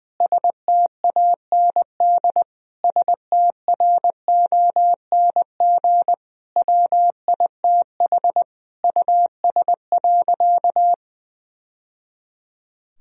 Morse code, heard as a keyed tone, about 20 wpm, 700 Hz. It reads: STAND STRONG WITH US.
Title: Sentences from Top 1000 Words - Review Part 3 - 20wpm